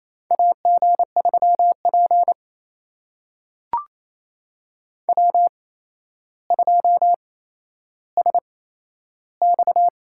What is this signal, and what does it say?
Morse code, heard as a keyed tone, about 28 wpm, spaced as for 12 wpm, 700 Hz. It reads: AG3P W2SX
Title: USA Call Signs - 4 Characters - 8x Character Spacing - 28wpm